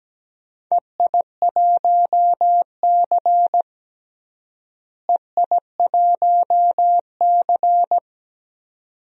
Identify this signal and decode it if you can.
Morse code, heard as a keyed tone, about 17 wpm, 700 Hz. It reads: EI1C EI1C
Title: DXCC-100 Call Signs - 4 Characters - Repeated 2x - 17wpm